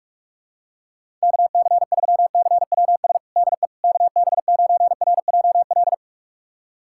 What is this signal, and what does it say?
Morse code, heard as a keyed tone, about 41 wpm, 700 Hz. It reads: KC3CWS DE KB9RJL
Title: USA Callsign DE Callsign - 6 Characters - 45wpm